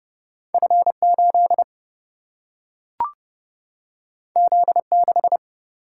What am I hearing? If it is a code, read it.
Morse code, heard as a keyed tone, about 30 wpm, 700 Hz. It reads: F8 Z6
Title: Single-Letter Single-Number 30wpm